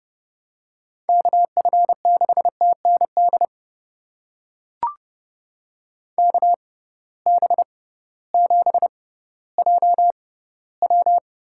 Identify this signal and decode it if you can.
Morse code, heard as a keyed tone, about 31 wpm, spaced as for 16 wpm, 700 Hz. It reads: KF6TND KB7JW
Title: USA Call Signs - 6 Characters - 6x Character Spacing - 30wpm